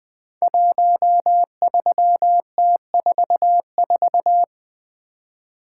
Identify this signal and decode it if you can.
Morse code, heard as a keyed tone, about 20 wpm, 700 Hz. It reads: 13T44
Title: Code Groups - 5 Letter:Numbers - 20wpm